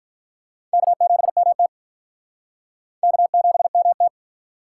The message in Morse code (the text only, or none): K7MT K7MT